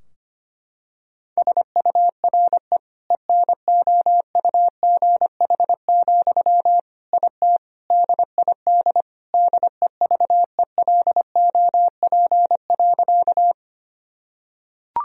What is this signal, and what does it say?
Morse code, heard as a keyed tone, about 25 wpm, 700 Hz. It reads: SURE ENOUGH, IT DID DEVELOP.
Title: Sentences from Top 1000 Words - Review Part 2 - 25wpm